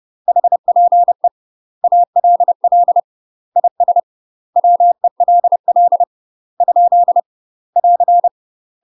Morse code, heard as a keyed tone, about 30 wpm, 700 Hz. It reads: HPE ALL IS WELL ? <AR>
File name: UK MASTER QSOs TRAINING / 30WPM / UK MASTER QSOs 0047